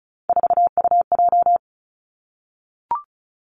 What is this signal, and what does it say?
Morse code, heard as a keyed tone, about 36 wpm, 700 Hz. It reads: 4UJ
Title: Three Challenging Letter-Number Combinations 35wpm